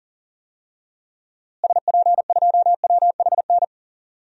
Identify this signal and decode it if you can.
Morse code, heard as a keyed tone, about 40 wpm, 700 Hz. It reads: SP2WHN
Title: DXCC-100 Call Signs - 6 Characters - Repeated 2x - 40wpm